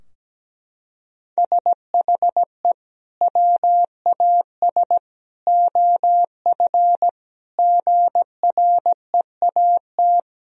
Morse code, heard as a keyed tone, about 17 wpm, 700 Hz. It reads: SHE WAS OF GREAT
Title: Sentences from Top 1000 Words - Review Part 4 - 17wpm